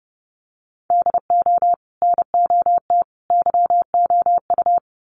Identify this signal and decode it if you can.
Morse code, heard as a keyed tone, about 30 wpm, 700 Hz. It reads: DO NOT YOU
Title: Sets of 3 Words 30wpm